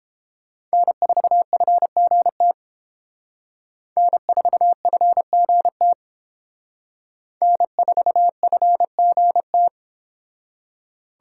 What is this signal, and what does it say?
Morse code, heard as a keyed tone, about 28 wpm, 700 Hz. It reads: N4FGT N4FGT N4FGT